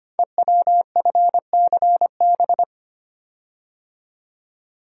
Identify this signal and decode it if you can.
Morse code, heard as a keyed tone, about 25 wpm, 690 Hz. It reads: EWFCB
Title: Code Groups - 5 Letters - 25wpm